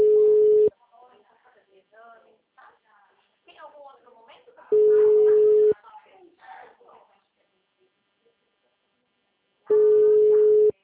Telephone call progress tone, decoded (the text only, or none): ringback tone